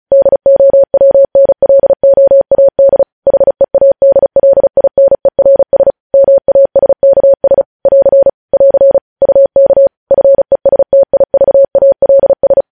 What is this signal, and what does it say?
Morse code, heard as a keyed tone, about 36 wpm, 550 Hz. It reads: DOWNLOAD HEADLINERS MASKS <AR> <AR> UK FESTIVALS
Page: news headlines in morse code at 35 wpm